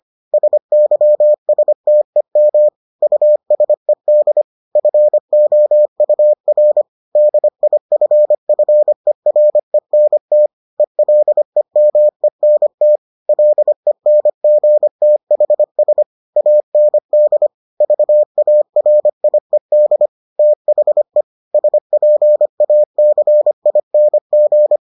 Morse code, heard as a keyed tone, about 25 wpm, 600 Hz. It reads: SYSTEM USED FOUR DIFFERENT ELEMENT LENGTHS AND VARIED THE SPACING